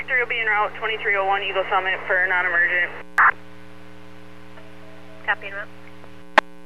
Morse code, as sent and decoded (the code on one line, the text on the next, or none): none